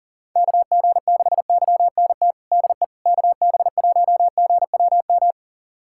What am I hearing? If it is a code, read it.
Morse code, heard as a keyed tone, about 40 wpm, 700 Hz. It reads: KG6YNT DE KB1GWM
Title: USA Callsign DE Callsign - 6 Characters - 40wpm